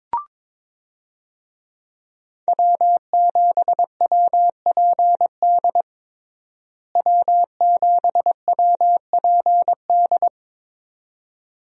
Morse code, heard as a keyed tone, about 22 wpm, 700 Hz. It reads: W7WPD W7WPD
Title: USA Call Signs - 5 Characters - Repeated 2x - 22wpm